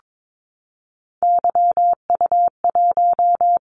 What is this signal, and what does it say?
Morse code, heard as a keyed tone, about 22 wpm, 700 Hz. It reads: YU1
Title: Callsign Prefixes for Top 25 DXCC - 22wpm